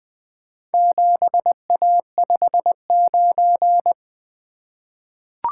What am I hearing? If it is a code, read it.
Morse code, heard as a keyed tone, about 20 wpm, 700 Hz. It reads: 7A59
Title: Four Letter-Number Combinations - 20wpm